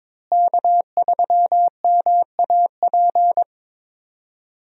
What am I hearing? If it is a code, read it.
Morse code, heard as a keyed tone, about 22 wpm, 700 Hz. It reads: K3MAP
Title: USA Call Signs - 5 Characters - Repeated 2x - 22wpm